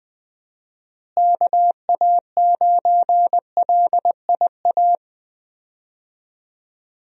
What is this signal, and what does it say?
Morse code, heard as a keyed tone, about 20 wpm, 700 Hz. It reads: KA9LIA
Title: USA Call Signs - 6 Characters - Repeated 2x - 20wpm